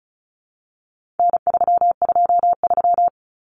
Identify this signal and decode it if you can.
Morse code, heard as a keyed tone, about 34 wpm, 700 Hz. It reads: N323
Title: Common Cut Numbers - 4-Digit 35wpm